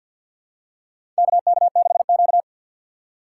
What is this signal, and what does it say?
Morse code, heard as a keyed tone, about 53 wpm, 700 Hz. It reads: KK6Q
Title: USA Call Signs - 2x Character Spacing - 50wpm